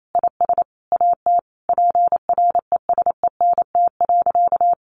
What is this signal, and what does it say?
Morse code, heard as a keyed tone, about 28 wpm, 700 Hz. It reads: IS AT PRESENT.
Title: Sentences from Top 1200 Words 28wpm